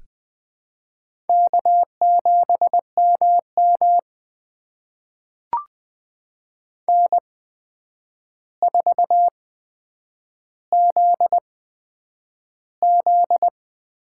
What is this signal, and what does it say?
Morse code, heard as a keyed tone, about 20 wpm, spaced as for 9 wpm, 700 Hz. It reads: K7MM N4ZZ
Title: USA Call Signs - 4 Characters - 8x Character Spacing - 20wpm